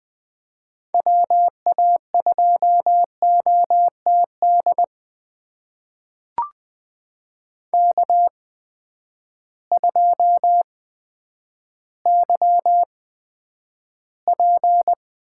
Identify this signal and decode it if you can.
Morse code, heard as a keyed tone, about 20 wpm, spaced as for 10 wpm, 700 Hz. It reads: WA2OTD K2YP
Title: USA Call Signs - 8x Character Spacing - 20wpm